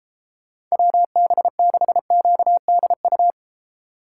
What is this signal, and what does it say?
Morse code, heard as a keyed tone, about 35 wpm, 700 Hz. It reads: WB6QDU